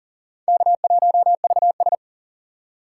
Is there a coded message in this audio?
Yes: Morse code from 0.5 to 2.0 s, heard as a keyed tone, about 51 wpm, 700 Hz. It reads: K1VS